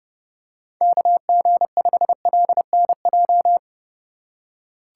Morse code, heard as a keyed tone, about 29 wpm, 700 Hz. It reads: KG5LNJ